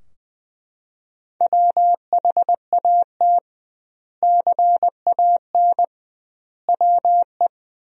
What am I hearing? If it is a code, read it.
Morse code, heard as a keyed tone, about 20 wpm, 700 Hz. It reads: WHAT CAN WE
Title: 2x Word-Spacing - Sentences from Top 100 Words - 20wpm